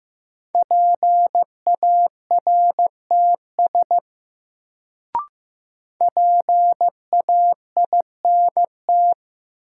Morse code, heard as a keyed tone, about 15 wpm, 700 Hz. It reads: PARTS PAINT
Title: Sentences from Top 1000 Words - Review Part 6 - New Word Drill - 15wpm